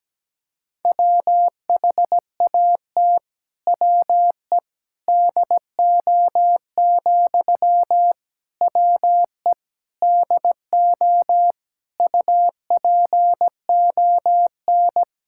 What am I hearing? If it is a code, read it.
Morse code, heard as a keyed tone, about 17 wpm, 700 Hz. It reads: WHAT WE DO, WE DO UPON
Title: Sentences from Top 1600 Words 17wpm